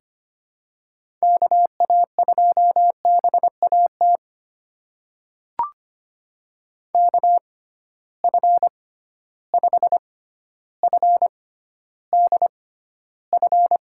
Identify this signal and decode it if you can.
Morse code, heard as a keyed tone, about 25 wpm, spaced as for 13 wpm, 700 Hz. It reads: KA2BAT KF5FDF
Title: USA Call Signs - 6 Characters - 6x Character Spacing - 25wpm